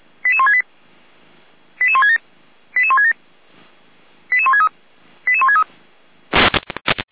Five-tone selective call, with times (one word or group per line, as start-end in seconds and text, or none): 0.3-0.6 88168
1.8-2.2 88167
2.8-3.1 88167
4.3-4.7 88163
5.3-5.6 88163